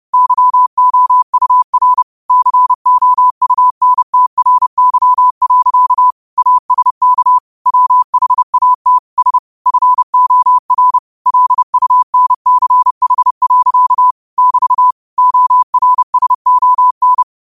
Morse code, heard as a keyed tone, about 30 wpm, 1000 Hz. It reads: YOUR COUNTRY. ASK WHAT S FOR LUNCH. <BT> ORSON
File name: 30wpm